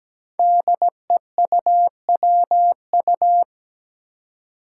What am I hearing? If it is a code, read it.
Morse code, heard as a keyed tone, about 17 wpm, 700 Hz. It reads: DEUWU
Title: Five Challenging Letter-Number Combinations 17wpm